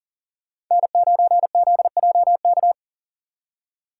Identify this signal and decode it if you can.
Morse code, heard as a keyed tone, about 37 wpm, 690 Hz. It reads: N9ZJK
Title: USA Call Signs - 4x Character Spacing - 40wpm